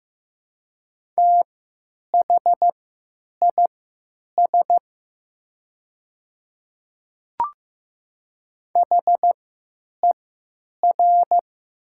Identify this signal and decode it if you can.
Morse code, heard as a keyed tone, about 15 wpm, spaced as for 9 wpm, 700 Hz. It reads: THIS HER
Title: 3x Character Spacing - Top 100 Words - 15wpm